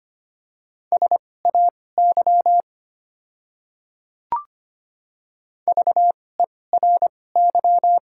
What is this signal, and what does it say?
Morse code, heard as a keyed tone, about 25 wpm, spaced as for 18 wpm, 700 Hz. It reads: SAY VERY
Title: 2x Character Spacing - Top 300 Words - 25wpm